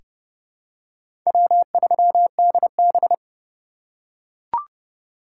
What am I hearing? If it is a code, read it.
Morse code, heard as a keyed tone, about 30 wpm, 700 Hz. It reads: W3DB
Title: USA Call Signs - 4x Character Spacing - 30wpm